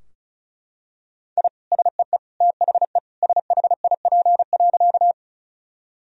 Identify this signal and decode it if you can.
Morse code, heard as a keyed tone, about 36 wpm, 700 Hz. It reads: I SEE THE SHIP.
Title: Sentences from Top 1000 Words - Review Part 2 - 35wpm